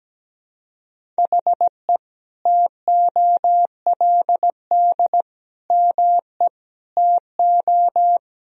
Morse code, heard as a keyed tone, about 17 wpm, 700 Hz. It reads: HE TOLD ME TO
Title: Sentences from Top 1000 Words - Review Part 1 - 17wpm